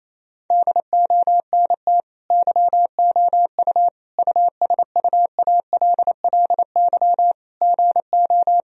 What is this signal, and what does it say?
Morse code, heard as a keyed tone, about 28 wpm, 700 Hz. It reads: DONT YOU USUALLY GO
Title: Sentences from Top 1000 Words - Review Part 3 - 28wpm